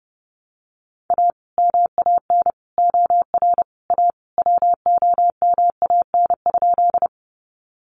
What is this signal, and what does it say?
Morse code, heard as a keyed tone, about 30 wpm, 700 Hz. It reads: A MAN OR A WOMAN?
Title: Sentences from Top 1100 Words 30wpm